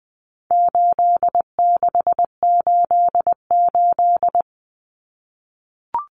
Morse code, heard as a keyed tone, about 20 wpm, 700 Hz. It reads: 8688
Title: Numbers - 4-Digit 20wpm